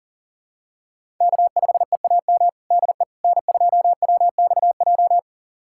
Morse code, heard as a keyed tone, about 37 wpm, 700 Hz. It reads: K5EAM DE N2WXJ
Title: USA Callsign DE Callsign - 5 Characters - 40wpm